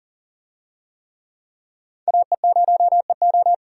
Morse code, heard as a keyed tone, about 40 wpm, 700 Hz. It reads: AE0EO
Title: USA Call Signs - 5 Characters - 40wpm